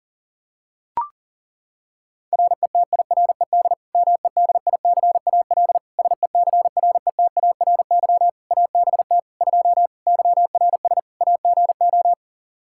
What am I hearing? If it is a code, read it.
Morse code, heard as a keyed tone, about 40 wpm, 700 Hz. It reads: RETIRED MEDICAL SECRETARY ABT 2 YRS AGO
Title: Practice QSO - Exchange 3 - More Info - 40wpm